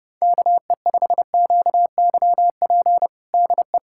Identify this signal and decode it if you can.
Morse code, heard as a keyed tone, about 30 wpm, 700 Hz. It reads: KE5QYP DE
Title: USA Callsign DE Callsign - 6 Characters - 30wpm